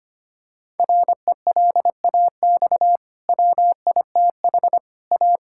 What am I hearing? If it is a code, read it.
Morse code, heard as a keyed tone, about 25 wpm, 700 Hz. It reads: RELAX WITH A
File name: Easy Sentences with Hints - 25wpm